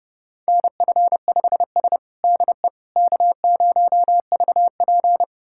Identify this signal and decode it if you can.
Morse code, heard as a keyed tone, about 30 wpm, 700 Hz. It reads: NF5S DE K0VP
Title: USA Callsign DE Callsign - 4 Characters - 30wpm